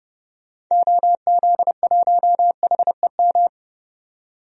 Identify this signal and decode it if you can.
Morse code, heard as a keyed tone, about 31 wpm, 700 Hz. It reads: OZ1HEM
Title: DXCC-100 Call Signs - 6 Characters - 30wpm